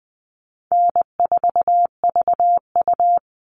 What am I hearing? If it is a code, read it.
Morse code, heard as a keyed tone, about 20 wpm, 700 Hz. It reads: N4VU
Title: All Cut Numbers - 4-Digit 20wpm